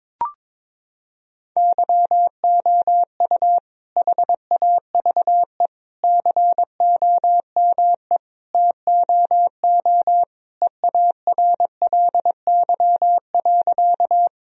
Morse code, heard as a keyed tone, about 22 wpm, 700 Hz. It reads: YOU HAVE COME TOO EARLY.
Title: Sentences from Top 1000 Words - Review Part 2 - 22wpm